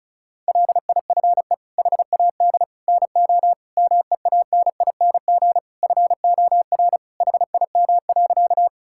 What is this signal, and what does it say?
Morse code, heard as a keyed tone, about 36 wpm, 700 Hz. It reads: LIFE HAD NO MEANING FOR HIM.
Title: Sentences from Top 1000 Words - Review Part 6 - 35wpm